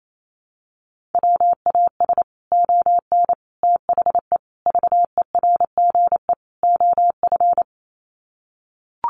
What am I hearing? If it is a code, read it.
Morse code, heard as a keyed tone, about 28 wpm, 700 Hz. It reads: WAS ON THE VERGE OF
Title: Sets of 5 Words Encore 28wpm